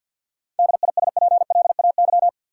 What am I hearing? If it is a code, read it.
Morse code, heard as a keyed tone, about 53 wpm, 700 Hz. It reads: DISPLAY
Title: Sentences from Top 1200 Words - New Word Drill - 50wpm